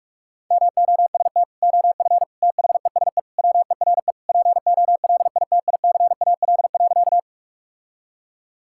Morse code, heard as a keyed tone, about 46 wpm, 700 Hz. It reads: MOST OF THESE WERE POLITICAL.
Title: Sentences from Top 1000 Words - Review Part 5 - 45wpm